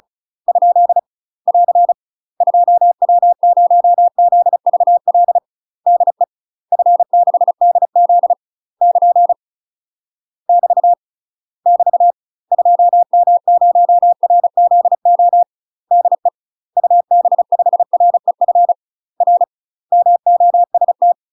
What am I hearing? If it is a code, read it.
Morse code, heard as a keyed tone, about 36 wpm, 700 Hz. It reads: ? <AR> 2W0ZVL DE F6DZ <KN> <BT> <BT> 2M0RZO DE UB5REF R MOST
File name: UK MASTER QSOs TRAINING / 35WPM / UK MASTER QSOs 0038